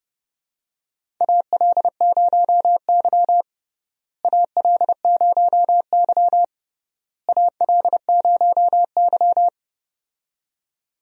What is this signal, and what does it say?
Morse code, heard as a keyed tone, about 30 wpm, 700 Hz. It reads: AL0Y AL0Y AL0Y
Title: USA Call Signs - 4 Characters - Repeated 3x - 30wpm